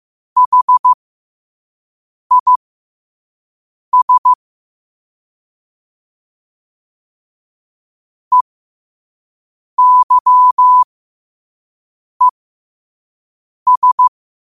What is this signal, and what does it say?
Morse code, heard as a keyed tone, about 15 wpm, spaced as for 5 wpm, 1000 Hz. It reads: HIS EYES
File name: qotdfiles / 5wpm